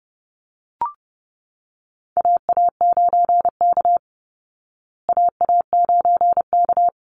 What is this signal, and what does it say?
Morse code, heard as a keyed tone, about 30 wpm, 700 Hz. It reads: AA9K AA9K
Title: Call Signs Repeated 3x - 30wpm